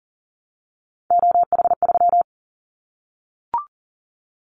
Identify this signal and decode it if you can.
Morse code, heard as a keyed tone, about 37 wpm, 700 Hz. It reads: OH3
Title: Callsign Prefixes for Top 25 DXCC - Repeated 3x - 40wpm